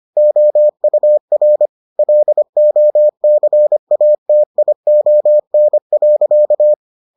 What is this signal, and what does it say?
Morse code, heard as a keyed tone, about 25 wpm, 600 Hz. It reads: OUR LOCATION.